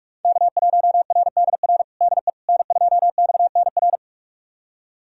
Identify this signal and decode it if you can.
Morse code, heard as a keyed tone, about 46 wpm, 700 Hz. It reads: K1RDR DE N2XNR
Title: USA Callsign DE Callsign - 5 Characters - 45wpm